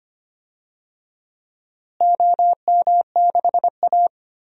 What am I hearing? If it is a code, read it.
Morse code, heard as a keyed tone, about 25 wpm, 700 Hz. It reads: OM6A
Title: DXCC-100 Call Signs - 4 Characters - Repeated 3x - 25wpm